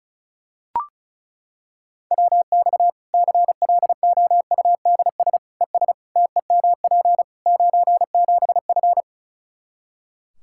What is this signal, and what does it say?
Morse code, heard as a keyed tone, about 36 wpm, 700 Hz. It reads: WX CLOUDS ES TEMP 97F
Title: Practice QSO - Exchange 2 - Info - 35wpm